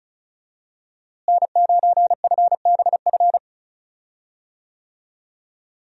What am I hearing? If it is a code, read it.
Morse code, heard as a keyed tone, about 36 wpm, 700 Hz. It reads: N9FBF
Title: USA Call Signs - 5 Characters - Repeated 2x - 35wpm